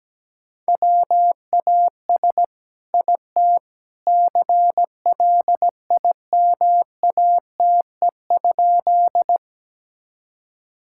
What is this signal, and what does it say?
Morse code, heard as a keyed tone, about 17 wpm, 700 Hz. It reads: WAS IT CLIMATE?
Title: Sentences from Top 1600 Words 17wpm